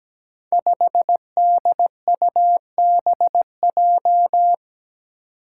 Morse code, heard as a keyed tone, about 17 wpm, 700 Hz. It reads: 5DUBJ
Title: Five Challenging Letter-Number Combinations 17wpm